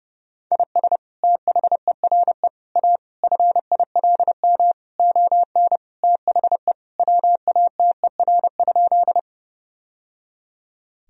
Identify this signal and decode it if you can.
Morse code, heard as a keyed tone, about 30 wpm, 700 Hz. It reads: IS THERE A FILM ON THE WATER?
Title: Sentences from Top 1000 Words - Review Part 3 - 30wpm